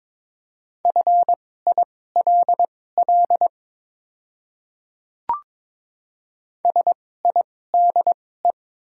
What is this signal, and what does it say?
Morse code, heard as a keyed tone, about 22 wpm, spaced as for 16 wpm, 700 Hz. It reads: FILL SIDE
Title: 2x Character Spacing - Top 500 Words - 22wpm